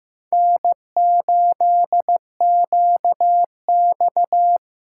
Morse code, heard as a keyed tone, about 15 wpm, 700 Hz. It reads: N8QX
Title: USA Call Signs - 4 Characters - 3x Character Spacing - 15wpm